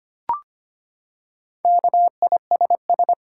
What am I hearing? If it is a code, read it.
Morse code, heard as a keyed tone, about 25 wpm, 700 Hz. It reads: KISS
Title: Four Letter Words - 25wpm